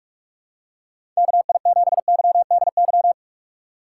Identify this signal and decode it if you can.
Morse code, heard as a keyed tone, about 46 wpm, 700 Hz. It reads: KI7YDY
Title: USA Call Signs - 6 Characters - 4x Character Spacing - 45wpm